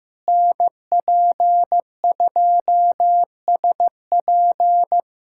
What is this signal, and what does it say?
Morse code, heard as a keyed tone, about 15 wpm, 700 Hz. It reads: NP2SP
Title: USA Callsign DE Callsign - 5 Characters - 15wpm